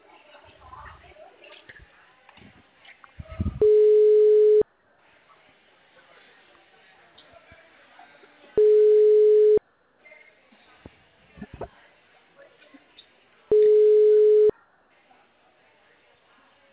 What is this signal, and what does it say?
Ringback tone (Europe, 425 Hz), with speech.
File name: out- -2089- - - .661